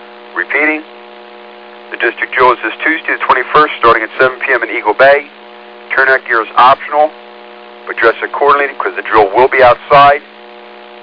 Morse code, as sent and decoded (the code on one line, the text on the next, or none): none